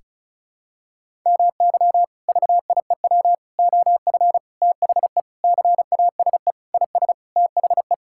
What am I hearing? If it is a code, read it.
Morse code, heard as a keyed tone, about 36 wpm, 700 Hz. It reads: MY VIEW OF THE CASE IS THE
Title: Sentences from Top 1000 Words - Review Part 6 - 35wpm